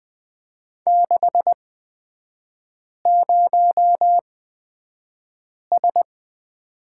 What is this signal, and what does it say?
Morse code, heard as a keyed tone, about 20 wpm, spaced as for 5 wpm, 700 Hz. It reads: 60S
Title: 243 Lesson 39e - Letter z - Three Characters 20wpm